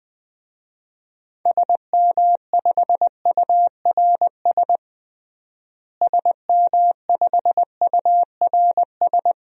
Morse code, heard as a keyed tone, about 20 wpm, 700 Hz. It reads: SM5URS SM5URS
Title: DXCC-100 Call Signs - 6 Characters - Repeated 3x - 20wpm